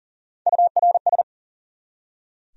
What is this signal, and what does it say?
Morse code, heard as a keyed tone, about 53 wpm, spaced as for 47 wpm, 700 Hz. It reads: URS
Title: Speed-Racing Abbreviations - 40wpm